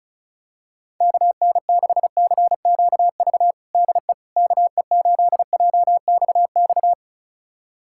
Morse code, heard as a keyed tone, about 36 wpm, 700 Hz. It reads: KN6CQV DE KE8JXX
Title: USA Callsign DE Callsign - 6 Characters - 35wpm